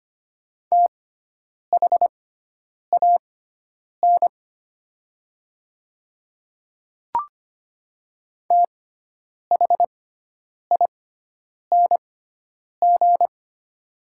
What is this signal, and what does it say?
Morse code, heard as a keyed tone, about 25 wpm, spaced as for 9 wpm, 700 Hz. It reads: THAN THING
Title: 6x Character Spacing - Top 300 Words - 25wpm